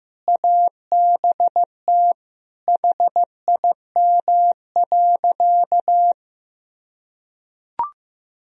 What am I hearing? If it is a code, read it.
Morse code, heard as a keyed tone, about 16 wpm, 700 Hz. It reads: ABT HIM.
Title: Sentences from Top 200 Words and Abbreviations 15wpm